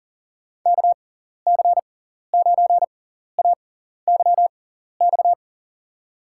Morse code, heard as a keyed tone, about 38 wpm, spaced as for 14 wpm, 700 Hz. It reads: KC9AYX
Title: USA Call Signs - 6 Characters - 6x Character Spacing - 40wpm